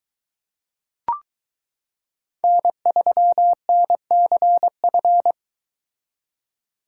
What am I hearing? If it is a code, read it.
Morse code, heard as a keyed tone, about 22 wpm, 700 Hz. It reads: N3NCF